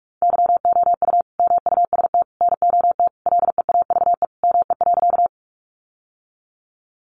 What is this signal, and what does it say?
Morse code, heard as a keyed tone, about 44 wpm, 700 Hz. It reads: YOU MUST NOT LEAVE ME.
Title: Sentences from Top 600 Words 45wpm